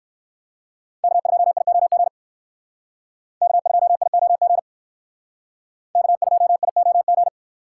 Morse code, heard as a keyed tone, about 58 wpm, 700 Hz. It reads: K2IOG K2IOG K2IOG